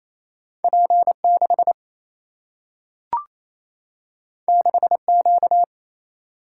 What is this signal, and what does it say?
Morse code, heard as a keyed tone, about 28 wpm, 700 Hz. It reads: P6 6Q
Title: Single-Letter Single-Number 28wpm